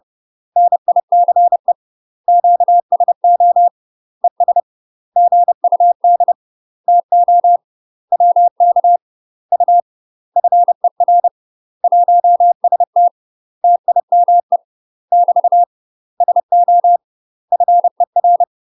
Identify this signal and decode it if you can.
Morse code, heard as a keyed tone, about 30 wpm, 700 Hz. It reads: NICE QSO ES GUD TO WK U FER 1ST TIME <BT> SO FER
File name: UK MASTER QSOs TRAINING / 30WPM / UK MASTER QSOs 0020